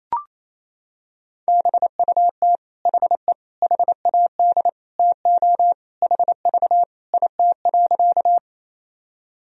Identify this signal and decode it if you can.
Morse code, heard as a keyed tone, about 28 wpm, 700 Hz. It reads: BUT HE HAD TO HV IT.
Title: Sentences from Top 100 Words and Abbreviations 28wpm